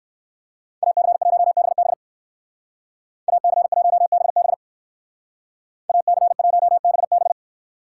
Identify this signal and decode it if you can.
Morse code, heard as a keyed tone, about 56 wpm, 700 Hz. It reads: AC1BB AC1BB AC1BB
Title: Speed Racing - USA 2x2 Call Signs 45wpm